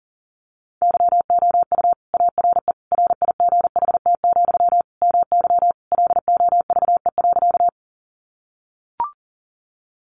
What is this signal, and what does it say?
Morse code, heard as a keyed tone, about 40 wpm, 700 Hz. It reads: YOU ARE RIGHT, MY LOVE.
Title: Sentences from Top 400 Words 40wpm